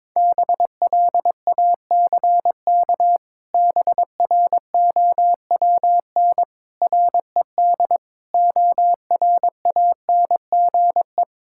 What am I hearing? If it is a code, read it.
Morse code, heard as a keyed tone, about 22 wpm, 700 Hz. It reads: BLACK BROWN RED ORANGE
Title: Pangram Challenge v2 - 22wpm